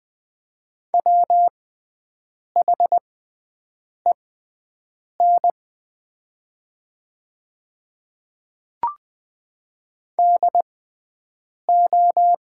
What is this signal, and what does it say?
Morse code, heard as a keyed tone, about 20 wpm, spaced as for 7 wpm, 700 Hz. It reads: WHEN DO
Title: 6x Character Spacing - Top 500 Words - 20wpm